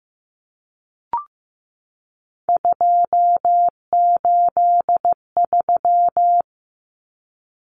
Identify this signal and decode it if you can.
Morse code, heard as a keyed tone, about 15 wpm, 700 Hz. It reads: 283
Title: Common Cut Numbers - 3-Digit 15wpm